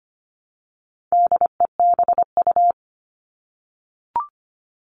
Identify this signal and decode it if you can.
Morse code, heard as a keyed tone, about 25 wpm, 700 Hz. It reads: DEBU
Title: All Cut Numbers - 4-Digit 25wpm